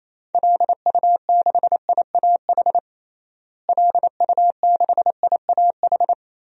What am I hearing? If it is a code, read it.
Morse code, heard as a keyed tone, about 27 wpm, 700 Hz. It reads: LU6IAH LU6IAH
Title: DXCC-100 Call Signs - 6 Characters - Repeated 2x - 28wpm